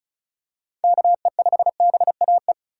Morse code, heard as a keyed tone, about 34 wpm, 700 Hz. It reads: KE5BAE
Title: USA Call Signs - 6 Characters - 4x Character Spacing - 35wpm